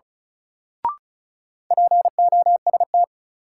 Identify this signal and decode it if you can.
Morse code, heard as a keyed tone, about 36 wpm, 700 Hz. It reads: POST
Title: Sentences from Top 1000 Words - New Word Drill - 35wpm